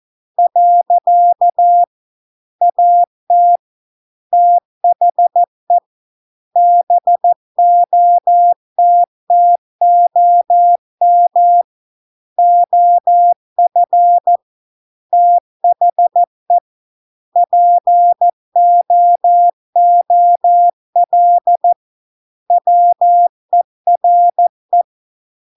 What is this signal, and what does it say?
Morse code, heard as a keyed tone, about 14 wpm, 700 Hz. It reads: . AT THE BOTTOM OF THE POOL WERE